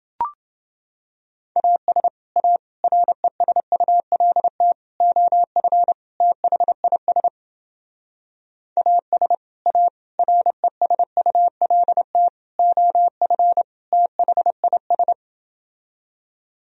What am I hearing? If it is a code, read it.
Morse code, heard as a keyed tone, about 30 wpm, 700 Hz. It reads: AS A RESULT OF THIS AS A RESULT OF THIS